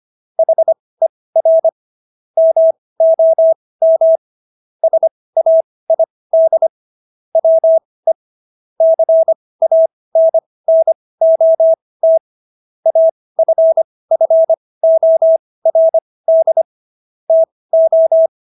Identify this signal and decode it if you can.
Morse code, heard as a keyed tone, about 25 wpm, spaced as for 18 wpm, 650 Hz. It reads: HER MOM SAID WE CANNOT AFFORD TO